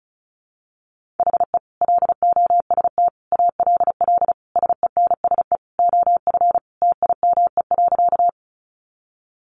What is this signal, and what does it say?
Morse code, heard as a keyed tone, about 34 wpm, 700 Hz. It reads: HE LOST ALL SENSE OF TIME.
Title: Sentences from Top 1400 Words 35wpm